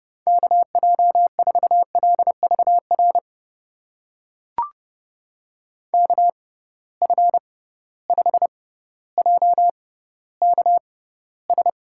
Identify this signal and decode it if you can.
Morse code, heard as a keyed tone, about 31 wpm, spaced as for 16 wpm, 700 Hz. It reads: KJ4LVR KF5JKS
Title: USA Call Signs - 6 Characters - 6x Character Spacing - 30wpm